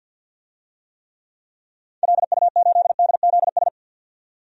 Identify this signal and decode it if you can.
Morse code, heard as a keyed tone, about 52 wpm, 700 Hz. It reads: LU8DZS
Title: DXCC-100 Call Signs - 6 Characters - 50wpm